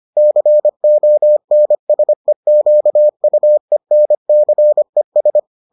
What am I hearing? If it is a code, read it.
Morse code, heard as a keyed tone, about 25 wpm, 600 Hz. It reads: CONSEQUENCES